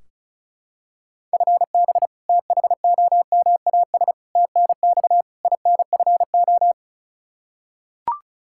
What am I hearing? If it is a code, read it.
Morse code, heard as a keyed tone, about 36 wpm, 700 Hz. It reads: FB THOMAS TNX INFO
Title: Practice QSO - Exchange 3 - More Info - 35wpm